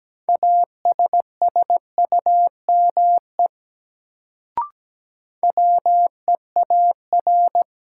Morse code, heard as a keyed tone, about 17 wpm, 700 Hz. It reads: ASSUME WEAR
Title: Sentences from Top 1500 Words - New Word Drill Alt - 17wpm